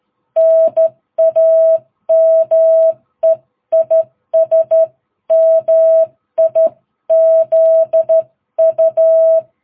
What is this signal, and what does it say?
Morse code, heard as a keyed tone, about 13 wpm, 660 Hz. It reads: NAME IS MIZU
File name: jh8cbh / 2022cw